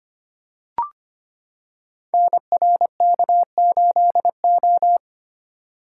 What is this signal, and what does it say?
Morse code, heard as a keyed tone, about 25 wpm, 700 Hz. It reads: NRK8O
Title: Code Groups - 5 Letter:Numbers - 25wpm